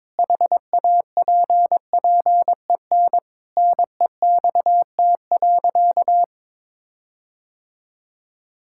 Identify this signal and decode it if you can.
Morse code, heard as a keyed tone, about 22 wpm, 700 Hz. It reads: HAPPEN NEXT.
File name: Sentences with 7-Word Phrases - 22wpm